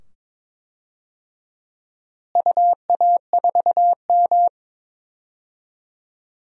Morse code, heard as a keyed tone, about 21 wpm, 700 Hz. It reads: UA4M
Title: DXCC-100 Call Signs - 4 Characters - Repeated 2x - 22wpm